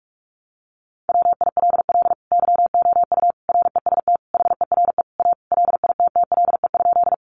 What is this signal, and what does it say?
Morse code, heard as a keyed tone, about 44 wpm, 700 Hz. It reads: WILL YOU REST HERE A LITTLE?
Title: Sentences from Top 500 Words 45wpm